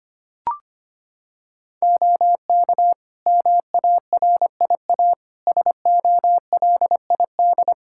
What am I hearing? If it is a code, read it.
Morse code, heard as a keyed tone, about 25 wpm, 700 Hz. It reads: OK MARIA SOLID
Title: Practice QSO - Exchange 3 - More Info - 25wpm